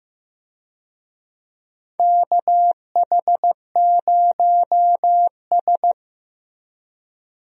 Morse code, heard as a keyed tone, about 15 wpm, 700 Hz. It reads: KH0S
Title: USA Call Signs - 4 Characters - Repeated 3x - 15wpm